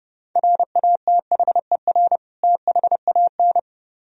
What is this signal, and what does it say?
Morse code, heard as a keyed tone, about 30 wpm, 700 Hz. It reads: RATHER THAN